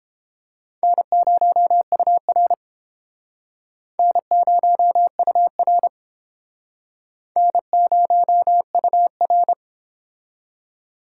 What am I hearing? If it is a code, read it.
Morse code, heard as a keyed tone, about 30 wpm, 700 Hz. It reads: N0UR N0UR N0UR